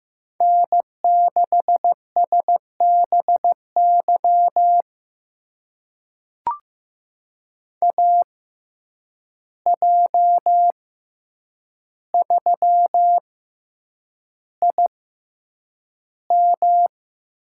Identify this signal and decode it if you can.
Morse code, heard as a keyed tone, about 15 wpm, spaced as for 8 wpm, 700 Hz. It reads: N6SBY AJ3IM
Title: USA Call Signs - 5 Characters - 6x Character Spacing - 15wpm